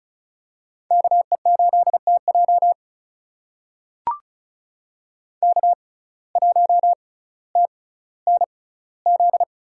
Morse code, heard as a keyed tone, about 34 wpm, spaced as for 18 wpm, 700 Hz. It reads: KE8TJ K1TNZ
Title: USA Call Signs - 5 Characters - 6x Character Spacing - 35wpm